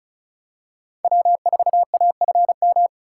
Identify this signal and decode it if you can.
Morse code, heard as a keyed tone, about 36 wpm, 700 Hz. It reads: W4AFM
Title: USA Call Signs - 3x Character Spacing - 35wpm